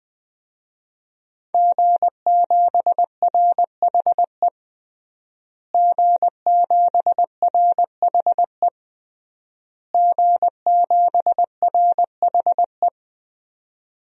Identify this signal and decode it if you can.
Morse code, heard as a keyed tone, about 20 wpm, 700 Hz. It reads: G7RHE G7RHE G7RHE
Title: DXCC-100 Call Signs - 5 Characters - Repeated 3x - 20wpm